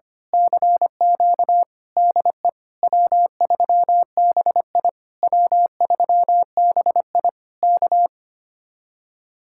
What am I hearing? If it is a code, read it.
Morse code, heard as a keyed tone, about 24 wpm, 700 Hz. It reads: CQ DE W3BI W3BI K
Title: Speed-Racing Common QSO Phrases - 25wpm